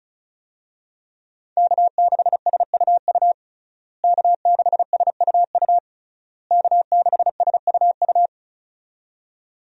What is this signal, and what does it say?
Morse code, heard as a keyed tone, about 36 wpm, 700 Hz. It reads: K6SUU K6SUU K6SUU
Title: USA Call Signs - 5 Characters - Repeated 3x - 35wpm